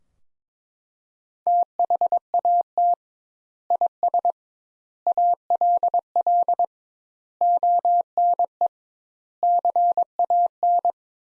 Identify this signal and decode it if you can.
Morse code, heard as a keyed tone, about 22 wpm, 700 Hz. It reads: THAT IS ALL ONE CAN
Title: Sentences Top 200 Words - 22wpm